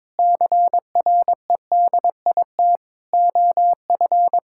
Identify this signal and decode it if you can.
Morse code, heard as a keyed tone, about 22 wpm, 700 Hz. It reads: CREDIT OF
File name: practice / 22wpm / Sentences with 8-Word Phrases - 22wpm